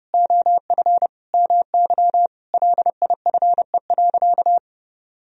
Morse code, heard as a keyed tone, about 30 wpm, 700 Hz. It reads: OF MY LIFE.
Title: Sentences from Top 1000 Words - Review Part 3 - 30wpm